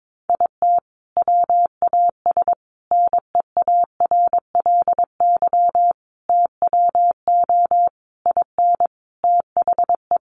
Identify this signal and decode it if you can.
Morse code, heard as a keyed tone, about 22 wpm, 700 Hz. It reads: IT WAS NEARLY TWO IN THE
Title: Sentences from Top 1300 Words 22wpm